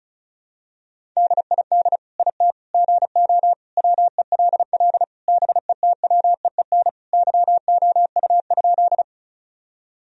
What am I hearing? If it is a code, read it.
Morse code, heard as a keyed tone, about 34 wpm, 700 Hz. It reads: DID IT GO WELL BETWEEN YOU?
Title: Sentences from Top 1000 Words - Review Part 1 - 35wpm